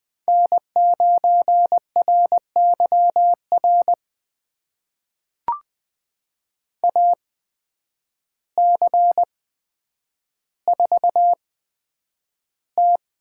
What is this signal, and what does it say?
Morse code, heard as a keyed tone, about 20 wpm, spaced as for 9 wpm, 700 Hz. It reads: N9RYR AC4T
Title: USA Call Signs - 5 Characters - 8x Character Spacing - 20wpm